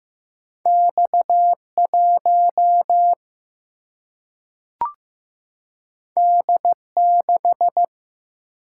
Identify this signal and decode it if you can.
Morse code, heard as a keyed tone, about 15 wpm, 700 Hz. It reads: X 1 D 6
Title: Single-Letter Single-Number 15wpm